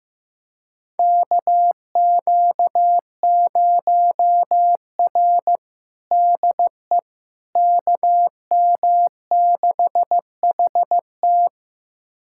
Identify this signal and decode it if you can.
Morse code, heard as a keyed tone, about 15 wpm, 700 Hz. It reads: KQ0R DE KM6HT
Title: USA Callsign DE Callsign 15wpm